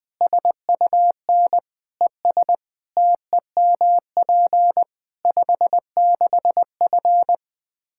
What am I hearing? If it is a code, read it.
Morse code, heard as a keyed tone, about 20 wpm, 700 Hz. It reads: SUN ES TEMP 56F
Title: Practice QSO - Exchange 2 - Info - 20wpm